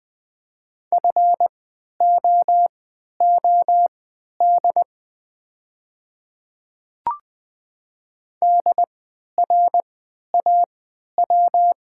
Morse code, heard as a keyed tone, about 20 wpm, spaced as for 11 wpm, 700 Hz. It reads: FOOD DRAW
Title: 3x Character Spacing - Top 300 Words - 20wpm